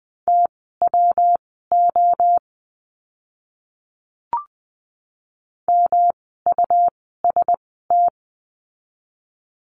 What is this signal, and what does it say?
Morse code, heard as a keyed tone, about 20 wpm, spaced as for 14 wpm, 700 Hz. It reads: TWO MUST
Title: 237 Lesson 38f - Letter q - Words 20wpm